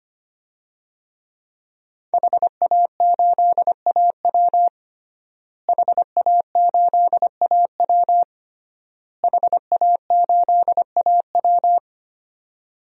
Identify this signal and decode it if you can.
Morse code, heard as a keyed tone, about 25 wpm, 700 Hz. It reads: HA8AW HA8AW HA8AW
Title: DXCC-100 Call Signs - 5 Characters - Repeated 3x - 25wpm